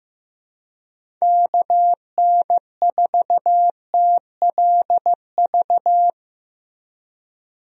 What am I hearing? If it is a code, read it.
Morse code, heard as a keyed tone, about 15 wpm, 700 Hz. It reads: KN4TLV
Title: USA Call Signs - 6 Characters - Repeated 2x - 15wpm